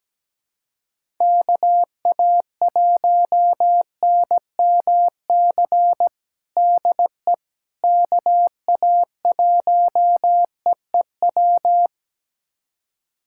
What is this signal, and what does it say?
Morse code, heard as a keyed tone, about 17 wpm, 700 Hz. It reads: KA1NMC DE KA1EEW
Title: USA Callsign DE Callsign 17wpm